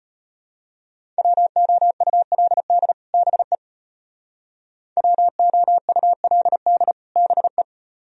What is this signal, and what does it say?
Morse code, heard as a keyed tone, about 32 wpm, 700 Hz. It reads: WOULD BE WOULD BE